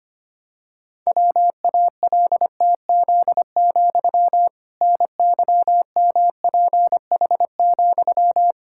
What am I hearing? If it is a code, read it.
Morse code, heard as a keyed tone, about 25 wpm, 700 Hz. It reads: WALTZ, NYMPH,
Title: Pangram Challenge v2 - 25wpm